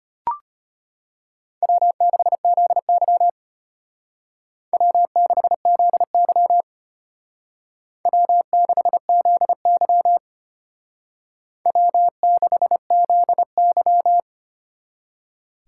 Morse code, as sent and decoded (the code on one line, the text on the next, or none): .-- -.... --.. -.-- / .-- -.... --.. -.-- / .-- -.... --.. -.-- / .-- -.... --.. -.--
W6ZY W6ZY W6ZY W6ZY